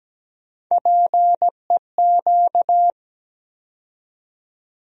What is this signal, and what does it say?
Morse code, heard as a keyed tone, about 17 wpm, 700 Hz. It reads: PEQ
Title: USA Call Sign Postfixes - 3 Letters 17wpm